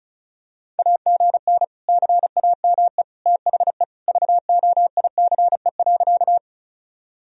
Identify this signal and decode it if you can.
Morse code, heard as a keyed tone, about 36 wpm, 700 Hz. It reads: AGN CAME THE VOICE.
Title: Sentences from Top 500 Words and Abbreviations 35wpm